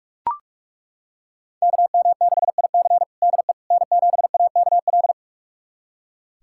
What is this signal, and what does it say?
Morse code, heard as a keyed tone, about 41 wpm, 700 Hz. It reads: KM6IC DE N7AKL
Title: USA Callsign DE Callsign - 5 Characters - 45wpm